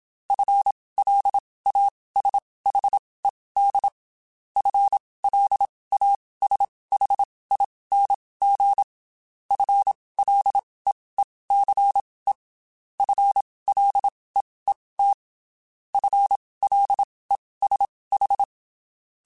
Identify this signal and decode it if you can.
Morse code, heard as a keyed tone, about 27 wpm, spaced as for 19 wpm, 800 Hz. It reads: FLASHED FLASHING FLEECE FLEET FLESH
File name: Spaced / FL Runs sp mp3s / FL Runs sp @25